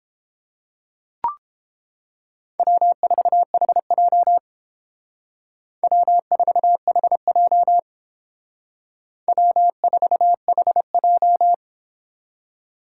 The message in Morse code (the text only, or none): W4HJ W4HJ W4HJ